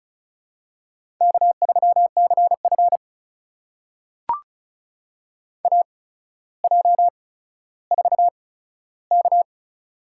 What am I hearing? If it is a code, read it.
Morse code, heard as a keyed tone, about 36 wpm, spaced as for 15 wpm, 700 Hz. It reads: K3CF AJ4K
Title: USA Call Signs - 4 Characters - 8x Character Spacing - 35wpm